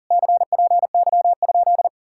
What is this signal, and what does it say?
Morse code, heard as a keyed tone, about 51 wpm, 700 Hz. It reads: CPY?